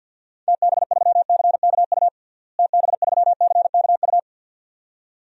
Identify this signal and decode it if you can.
Morse code, heard as a keyed tone, about 52 wpm, 700 Hz. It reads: TB3CKU TB3CKU
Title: DXCC-100 Call Signs - 6 Characters - Repeated 2x - 50wpm